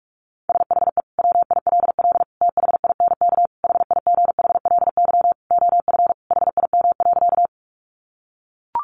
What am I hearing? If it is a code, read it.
Morse code, heard as a keyed tone, about 44 wpm, 700 Hz. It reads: SHE WILL THINK HIGHLY OF HIM.
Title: Sentences from Top 1100 Words 45wpm